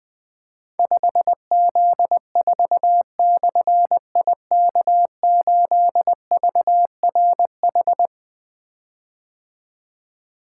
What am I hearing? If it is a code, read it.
Morse code, heard as a keyed tone, about 20 wpm, 700 Hz. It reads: 5Z4/IK8VRH
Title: Difficult Call Signs 20wpm